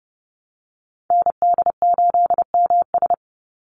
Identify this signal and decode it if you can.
Morse code, heard as a keyed tone, about 30 wpm, 700 Hz. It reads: ND8MS
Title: Call Signs Repeated 2x - 30wpm